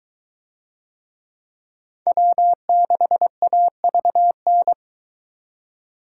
Morse code, heard as a keyed tone, about 21 wpm, 700 Hz. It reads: W6AVN